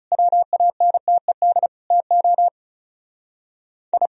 Morse code, heard as a keyed tone, about 38 wpm, 700 Hz. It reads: WANTED TO I